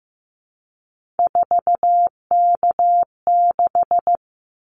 Morse code, heard as a keyed tone, about 15 wpm, 700 Hz. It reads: 4K6
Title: Callsign Prefixes for Top 100 DXCC - 15wpm